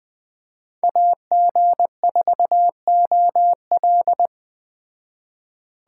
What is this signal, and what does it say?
Morse code, heard as a keyed tone, about 20 wpm, 700 Hz. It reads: AG4OL
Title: Speed Racing - USA 2x2 Call Signs 17wpm